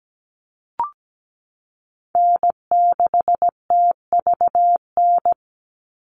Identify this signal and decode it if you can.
Morse code, heard as a keyed tone, about 17 wpm, 700 Hz. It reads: N6TVN
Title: Call Signs - 17wpm